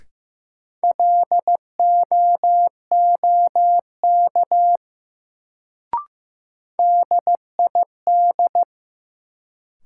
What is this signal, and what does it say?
Morse code, heard as a keyed tone, about 15 wpm, 700 Hz. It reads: LOOK DID
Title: Sets of 2 Words - New Word Drill - 15wpm